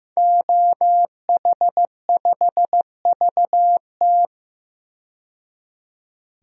Morse code, heard as a keyed tone, about 15 wpm, 700 Hz. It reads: OH5VT
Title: DXCC-100 Call Signs - 5 Characters - 15wpm